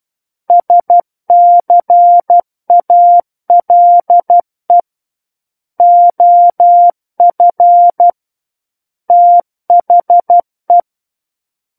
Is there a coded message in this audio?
Yes: Morse code from 0.5 to 10.8 s, heard as a keyed tone, about 12 wpm, 700 Hz. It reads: SCALE OF THE